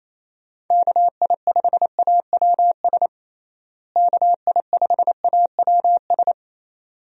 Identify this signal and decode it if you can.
Morse code, heard as a keyed tone, about 28 wpm, 700 Hz. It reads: KI5AWS KI5AWS
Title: USA Call Signs - 6 Characters - Repeated 2x - 28wpm